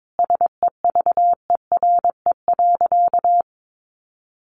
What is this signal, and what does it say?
Morse code, heard as a keyed tone, about 22 wpm, 700 Hz. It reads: SEVERE.
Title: Sentences from Top 1400 Words 22wpm